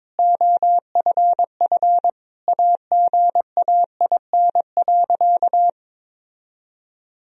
Morse code, heard as a keyed tone, about 22 wpm, 700 Hz. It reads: OFF AGAIN.
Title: Sentences from Top 1000 Words - Review Part 2 - 22wpm